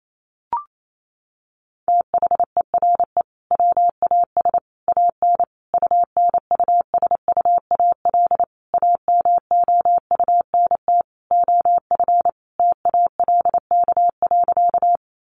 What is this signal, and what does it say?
Morse code, heard as a keyed tone, about 28 wpm, 700 Hz. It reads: THERE WAS AN UNUSUAL AMOUNT OF TALK.
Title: Sentences from Top 1400 Words 28wpm